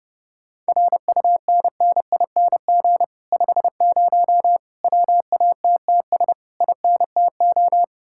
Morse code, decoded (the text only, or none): RUNNING 50 WATTS INTO